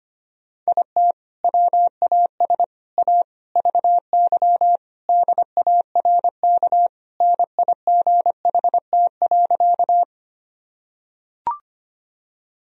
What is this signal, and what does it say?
Morse code, heard as a keyed tone, about 25 wpm, 700 Hz. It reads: IT WAS A VY DARK NIGHT.
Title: Sentences from Top 500 Words and Abbreviations 25wpm